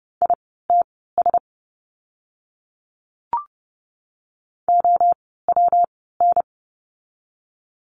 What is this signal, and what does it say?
Morse code, heard as a keyed tone, about 30 wpm, spaced as for 17 wpm, 700 Hz. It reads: ITS OWN
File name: course / 30:10wpm / 115 Lesson 19f - Letter w - Words 30wpm